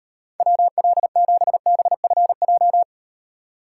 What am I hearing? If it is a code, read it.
Morse code, heard as a keyed tone, about 38 wpm, 700 Hz. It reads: WL7BFJ